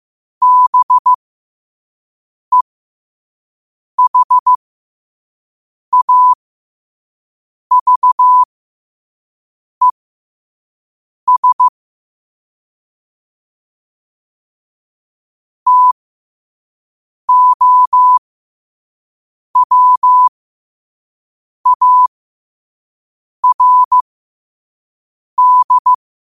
Morse code, heard as a keyed tone, about 15 wpm, spaced as for 5 wpm, 1000 Hz. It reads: BEHAVES TOWARD
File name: qotdfiles / 5wpm